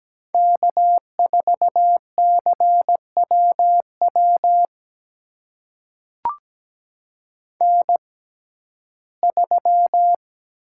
Morse code, heard as a keyed tone, about 17 wpm, spaced as for 12 wpm, 700 Hz. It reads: K4CWW N3
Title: USA Call Signs - 5 Characters - 6x Character Spacing - 17wpm